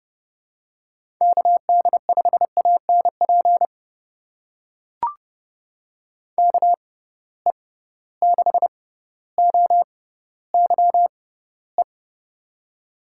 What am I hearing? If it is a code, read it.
Morse code, heard as a keyed tone, about 32 wpm, spaced as for 16 wpm, 700 Hz. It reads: KD5ANP KE6OYE
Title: USA Call Signs - 6 Characters - 6x Character Spacing - 30wpm